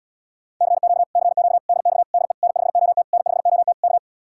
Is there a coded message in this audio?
Yes: Morse code from 0.6 to 4.0 s, heard as a keyed tone, about 79 wpm, 700 Hz. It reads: CQ CQ DX DE N5OT N5OT K